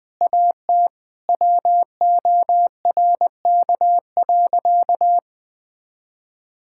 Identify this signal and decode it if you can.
Morse code, heard as a keyed tone, about 20 wpm, 700 Hz. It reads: AT WORK.
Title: Difficult Sentences with Hints - 20wpm